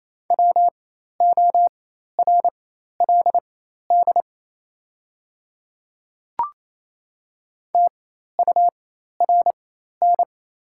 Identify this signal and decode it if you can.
Morse code, heard as a keyed tone, about 28 wpm, spaced as for 13 wpm, 700 Hz. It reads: WORLD TURN
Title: 4x Character Spacing - Top 300 Words - 28wpm